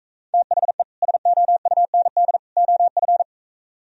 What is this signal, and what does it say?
Morse code, heard as a keyed tone, about 44 wpm, 700 Hz. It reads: THE SOUND OF